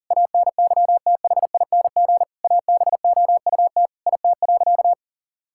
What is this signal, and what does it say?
Morse code, heard as a keyed tone, about 40 wpm, 700 Hz. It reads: ANYTHING ABOUT IT.
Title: Sentences from Top 1600 Words 40wpm